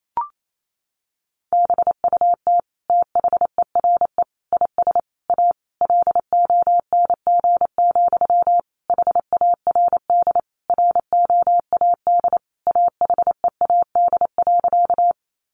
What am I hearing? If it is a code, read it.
Morse code, heard as a keyed tone, about 28 wpm, 700 Hz. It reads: BUT THERE IS A LONG, HARD ROAD AHEAD.
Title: Sentences from Top 1100 Words 28wpm